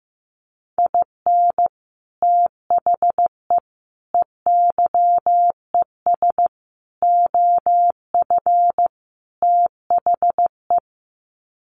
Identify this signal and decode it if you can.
Morse code, heard as a keyed tone, about 15 wpm, 700 Hz. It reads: IN THE EYES OF THE
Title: Sets of 5 Words 15wpm